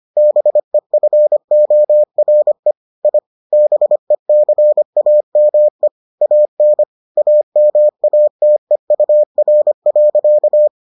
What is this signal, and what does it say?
Morse code, heard as a keyed tone, about 25 wpm, 600 Hz. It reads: BEFORE I BECAME AN AMATEUR.